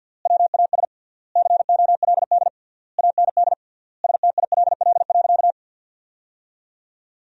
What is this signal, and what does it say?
Morse code, heard as a keyed tone, about 50 wpm, 700 Hz. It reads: WAS COLD AND STILL.
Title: 3x Word-Spacing - Sentences from Top 400 Words - 50wpm